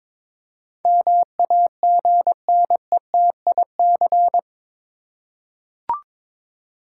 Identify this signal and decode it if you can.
Morse code, heard as a keyed tone, about 22 wpm, 700 Hz. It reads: MAGNETIC